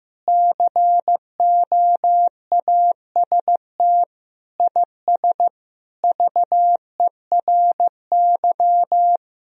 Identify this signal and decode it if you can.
Morse code, heard as a keyed tone, about 16 wpm, 700 Hz. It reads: COAST IS VERY
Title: Sentences from Top 1000 Words - Review Part 4 - 15wpm